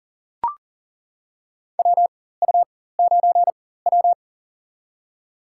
Morse code, heard as a keyed tone, about 39 wpm, spaced as for 19 wpm, 700 Hz. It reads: WU9W
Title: USA Call Signs - 4 Characters - 4x Character Spacing - 40wpm